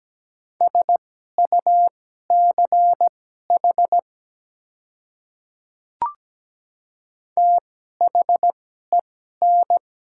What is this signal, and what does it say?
Morse code, heard as a keyed tone, about 17 wpm, spaced as for 12 wpm, 700 Hz. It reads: SUCH THEN
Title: 2x Character Spacing - Top 200 Words - 17wpm